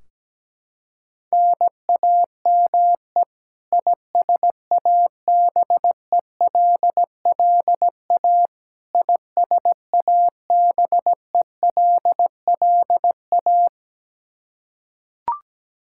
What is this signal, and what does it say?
Morse code, heard as a keyed tone, about 17 wpm, 700 Hz. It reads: NAME ISABELLA ISABELLA
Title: K1USN SST QSO Practice 17wpm